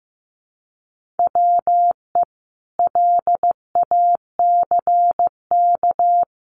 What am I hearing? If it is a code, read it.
Morse code, heard as a keyed tone, about 15 wpm, 700 Hz. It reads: WE LACK
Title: Sentences from Top 1400 Words 15wpm